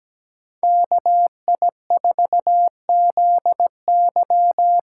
Morse code, heard as a keyed tone, about 17 wpm, 700 Hz. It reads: KI4ZY
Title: USA Call Signs - 6 Characters - 6x Character Spacing - 17wpm